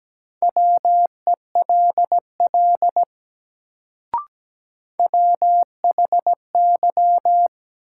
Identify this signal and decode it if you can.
Morse code, heard as a keyed tone, about 17 wpm, 700 Hz. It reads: WELL WHY
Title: Sentences from Top 200 Words - New Word Drill - 17wpm